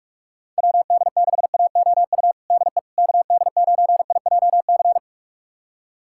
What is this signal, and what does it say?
Morse code, heard as a keyed tone, about 46 wpm, 690 Hz. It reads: WD6AOU DE KD9IJC
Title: USA Callsign DE Callsign - 6 Characters - 45wpm